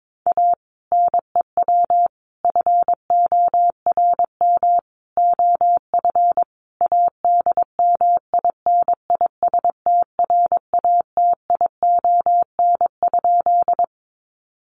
Morse code, heard as a keyed tone, about 22 wpm, 700 Hz. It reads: A NEW FORM OF ADMINISTRATION?
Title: Sentences from Top 1300 Words 22wpm